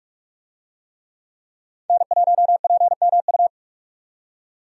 Morse code, heard as a keyed tone, about 46 wpm, 700 Hz. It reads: N1PMU
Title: USA Call Signs - 5 Characters - 45wpm